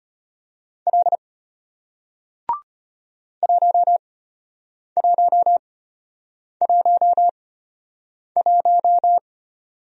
Morse code, heard as a keyed tone, about 34 wpm, spaced as for 8 wpm, 700 Hz. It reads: L 1111